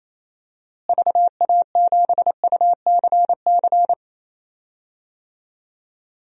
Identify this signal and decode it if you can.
Morse code, heard as a keyed tone, about 27 wpm, 700 Hz. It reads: VA7UCC
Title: DXCC-100 Call Signs - 6 Characters - Repeated 2x - 28wpm